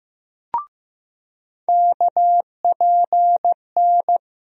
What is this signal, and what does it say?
Morse code, heard as a keyed tone, about 15 wpm, 700 Hz. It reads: KPN
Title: USA Call Sign Postfixes - 3 Letters 15wpm